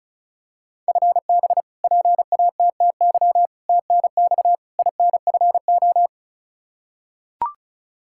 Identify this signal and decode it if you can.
Morse code, heard as a keyed tone, about 36 wpm, 700 Hz. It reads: FB PATTY TNX INFO
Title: K1USN SST QSO Practice 35wpm